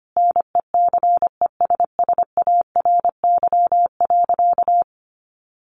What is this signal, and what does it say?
Morse code, heard as a keyed tone, about 25 wpm, 700 Hz. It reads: NECESSARY.
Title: Sentences from Top 1500 Words 25wpm